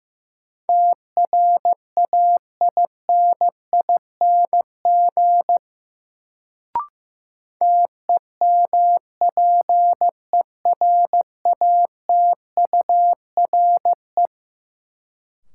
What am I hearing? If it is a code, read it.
Morse code, heard as a keyed tone, about 16 wpm, 700 Hz. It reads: TRAINING TEMPERATURE
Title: Sentences from Top 1000 Words - Review Part 4 - New Word Drill - 15wpm